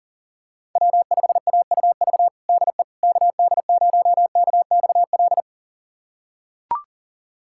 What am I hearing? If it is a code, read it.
Morse code, heard as a keyed tone, about 40 wpm, 700 Hz. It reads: W5AUV DE KD0KXL
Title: USA Callsign DE Callsign 40wpm